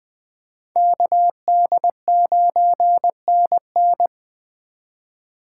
Morse code, heard as a keyed tone, about 20 wpm, 700 Hz. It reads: KD9NN
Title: Speed Racing - USA Call Signs 17wpm